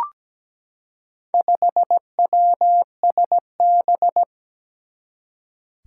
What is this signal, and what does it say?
Morse code, heard as a keyed tone, about 17 wpm, 700 Hz. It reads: 5WSB
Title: Four Challenging Letter-Number Combinations 17wpm